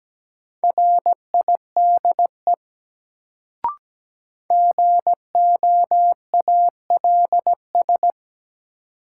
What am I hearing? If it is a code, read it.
Morse code, heard as a keyed tone, about 17 wpm, 700 Hz. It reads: RIDE GOALS
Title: Sentences from Top 1000 Words - Review Part 6 - New Word Drill - 17wpm